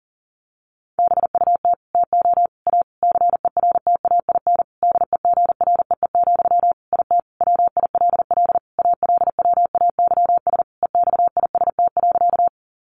Morse code, heard as a keyed tone, about 40 wpm, 700 Hz. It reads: BUT TO A CERTAIN DEGREE, IT WILL ALWAYS EXIST.
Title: Sentences from Top 1500 Words 40wpm